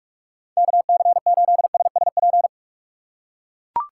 Morse code, heard as a keyed tone, about 42 wpm, 700 Hz. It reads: KC8SSP
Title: USA Call Signs - 6 Characters - 3x Character Spacing - 45wpm